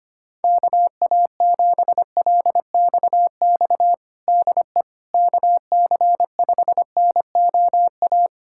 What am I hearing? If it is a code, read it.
Morse code, heard as a keyed tone, about 25 wpm, 700 Hz. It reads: KA7LXX DE KC5NOA
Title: USA Callsign DE Callsign 25wpm